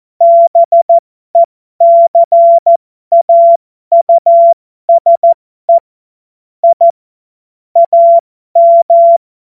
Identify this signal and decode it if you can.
Morse code, heard as a keyed tone, about 14 wpm, spaced as for 12 wpm, 680 Hz. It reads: BECAUSE I AM
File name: Morsekurs / Witze